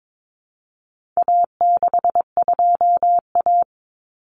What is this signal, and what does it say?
Morse code, heard as a keyed tone, about 22 wpm, 700 Hz. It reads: A62A
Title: Common Cut Numbers - 4-Digit 22wpm